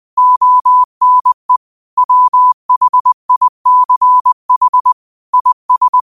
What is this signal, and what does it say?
Morse code, heard as a keyed tone, about 20 wpm, 1000 Hz. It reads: ONE WHICH IS
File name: qotdfiles / 20wpm